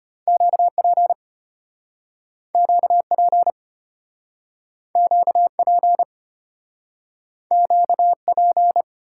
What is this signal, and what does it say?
Morse code, heard as a keyed tone, about 33 wpm, 700 Hz. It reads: QP QP QP QP